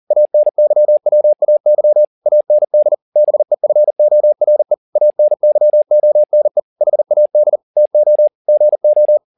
Morse code, heard as a keyed tone, about 40 wpm, 600 Hz. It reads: ANYWAY AND BEFORE ANYONE HAD TO GO
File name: the old man and the sea 2 40wpm